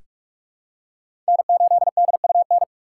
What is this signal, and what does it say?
Morse code, heard as a keyed tone, about 43 wpm, 700 Hz. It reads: N8DUN
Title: USA Call Signs - 4x Character Spacing - 45wpm